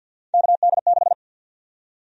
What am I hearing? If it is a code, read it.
Morse code, heard as a keyed tone, about 50 wpm, 700 Hz. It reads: KD6
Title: USA Call Sign Prefixes - 50wpm